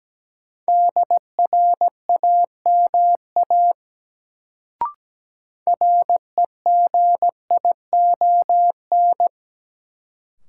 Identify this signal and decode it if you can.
Morse code, heard as a keyed tone, about 17 wpm, 700 Hz. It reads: DRAMA REGION
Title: Sentences from Top 1500 Words - New Word Drill Alt - 17wpm